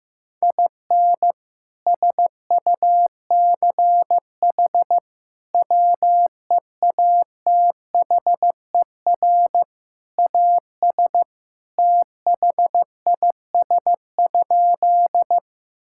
Morse code, heard as a keyed tone, about 15 wpm, 700 Hz. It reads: IN SUCH WEATHER AS THIS?
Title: Sentences from Top 1000 Words - Review Part 5 - 15wpm